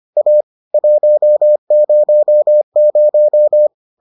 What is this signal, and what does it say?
Morse code, heard as a keyed tone, about 25 wpm, 600 Hz. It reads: A 100